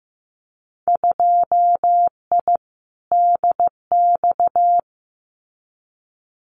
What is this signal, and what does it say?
Morse code, heard as a keyed tone, about 15 wpm, 700 Hz. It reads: 2I DX
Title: Winter Field Day Exchanges 15wpm